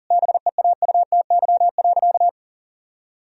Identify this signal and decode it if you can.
Morse code, heard as a keyed tone, about 40 wpm, 700 Hz. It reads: BEAUTY.
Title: Sentences from Top 1000 Words - Review Part 2 - 40wpm